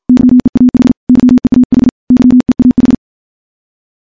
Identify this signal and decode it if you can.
Morse code, heard as a keyed tone, about 51 wpm, 250 Hz. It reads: YEAH YEAH YEAH
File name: morse / morse-training-50WPM